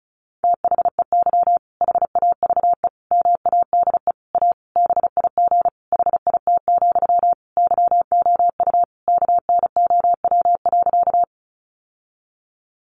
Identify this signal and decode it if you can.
Morse code, heard as a keyed tone, about 34 wpm, 700 Hz. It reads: THEY HAVE MADE A BIG HIT, YOU KNOW.
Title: Sentences from Top 1200 Words 35wpm